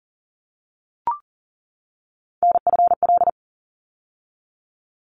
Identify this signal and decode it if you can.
Morse code, heard as a keyed tone, about 37 wpm, 700 Hz. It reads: NFL
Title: ARRL Sections 40wpm